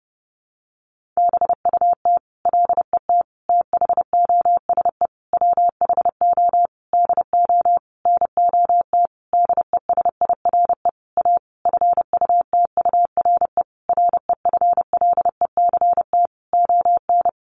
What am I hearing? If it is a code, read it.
Morse code, heard as a keyed tone, about 30 wpm, 700 Hz. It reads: BUT LET THOSE WHO DO NOT DESIRE A FUTURE REFLECT ON
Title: Sentences from Top 1500 Words 30wpm